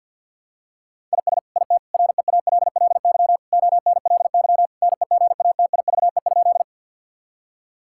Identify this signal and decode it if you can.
Morse code, heard as a keyed tone, about 50 wpm, 700 Hz. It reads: IS IT REALLY ONLY NEGATIVE?
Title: Sentences from Top 1000 Words - Review Part 6 - 50wpm